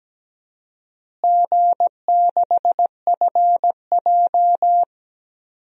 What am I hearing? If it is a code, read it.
Morse code, heard as a keyed tone, about 17 wpm, 700 Hz. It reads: G6FJ
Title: DXCC-100 Call Signs - 4 Characters - Repeated 3x - 17wpm